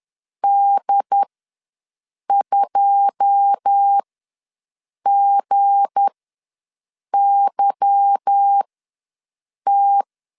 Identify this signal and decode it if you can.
Morse code, heard as a keyed tone, about 10 wpm, spaced as for 6 wpm, 800 Hz. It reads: D2GYT
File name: LEZIONI / 9 286 ws 10wpm 4032